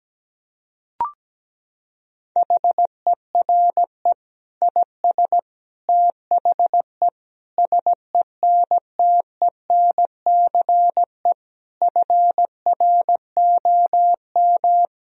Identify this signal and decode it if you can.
Morse code, heard as a keyed tone, about 17 wpm, 700 Hz. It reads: HERE IS THE SENTENCE FROM
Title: Sentences from Top 1000 Words - Review Part 1 - 17wpm